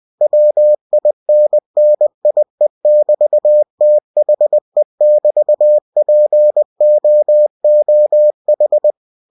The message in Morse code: WINNIE<BT>THE<BT>POOH

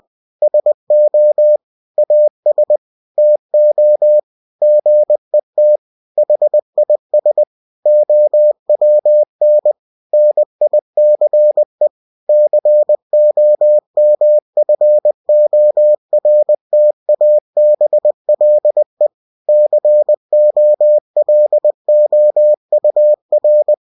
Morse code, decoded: SO AS TO GET HIS OWN NICE COMFORTABLE COLOUR

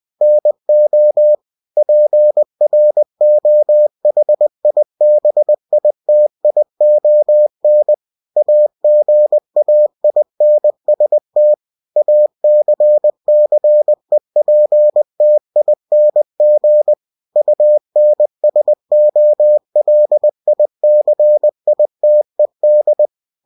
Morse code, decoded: NO PROHIBITION AGAINST ACCEPTING UNSOLICITED

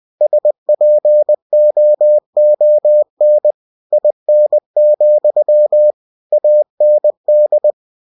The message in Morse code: SPOON IN, AND

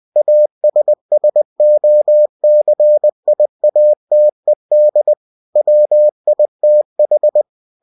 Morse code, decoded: ASSOCIATED WITH

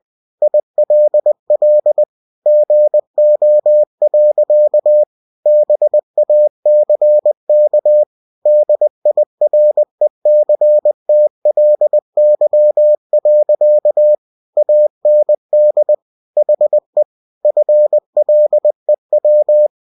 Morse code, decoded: ILL GO. BACK DIRECTLY. AND HE FLEW

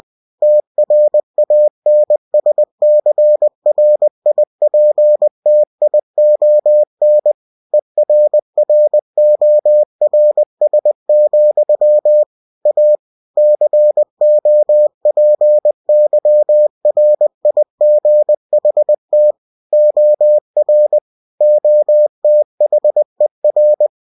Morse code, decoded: TRANSCRIPTION ERRORS, A COPYRIGHT OR OTHER

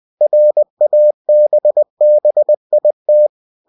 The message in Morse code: RABBIT

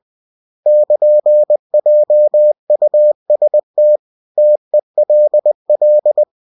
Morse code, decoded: <KN>JUST TELL

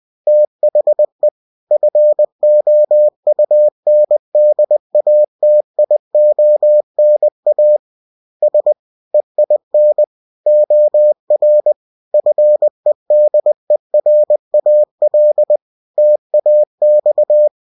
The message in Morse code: THE FOUNDATIONAS EIN OR FEDERAL TAX

